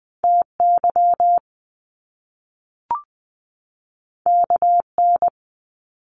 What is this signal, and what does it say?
Morse code, heard as a keyed tone, about 20 wpm, 700 Hz. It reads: T Y K N
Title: Top Two Letter Combinations in Words 20wpm